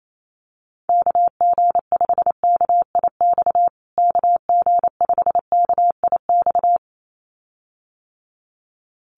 Morse code, heard as a keyed tone, about 28 wpm, 700 Hz. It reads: KG5KIX KG5KIX
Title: USA Call Signs Repeated 2x 28wpm